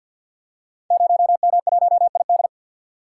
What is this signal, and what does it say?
Morse code, heard as a keyed tone, about 47 wpm, 700 Hz. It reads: 0M1ID
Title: Code Groups - 5 Letter:Numbers - 50wpm